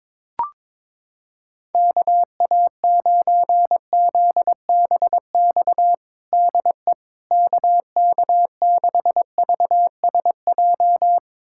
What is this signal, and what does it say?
Morse code, heard as a keyed tone, about 22 wpm, 700 Hz. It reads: KA9ZBX DE KK6VSJ
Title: USA Callsign DE Callsign - 6 Characters - 22wpm